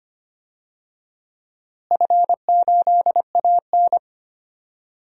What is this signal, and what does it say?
Morse code, heard as a keyed tone, about 25 wpm, 700 Hz. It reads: F8AN
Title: DXCC-100 Call Signs - 4 Characters - Repeated 2x - 25wpm